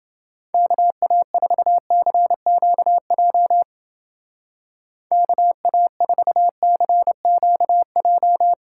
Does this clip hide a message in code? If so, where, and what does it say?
Yes: Morse code from 0.5 to 8.5 s, heard as a keyed tone, about 29 wpm, 700 Hz. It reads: KA4CQJ KA4CQJ